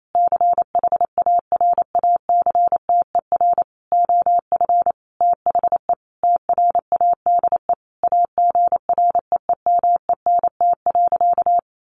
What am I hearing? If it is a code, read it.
Morse code, heard as a keyed tone, about 28 wpm, 700 Hz. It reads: CHARACTER OF THE TRADE AGREEMENT.
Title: Sentences from Top 1500 Words 28wpm